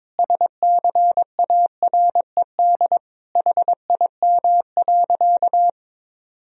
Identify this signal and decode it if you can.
Morse code, heard as a keyed tone, about 22 wpm, 700 Hz. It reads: SCARED HIM.
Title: Sentences from Top 1000 Words - Review Part 4 - 22wpm